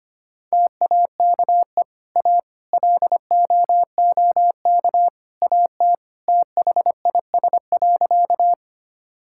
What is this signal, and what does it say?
Morse code, heard as a keyed tone, about 25 wpm, 700 Hz. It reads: TAKE A LOOK AT THIS.
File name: practice / 25wpm / Sentences with 3-Word Phrases - 25wpm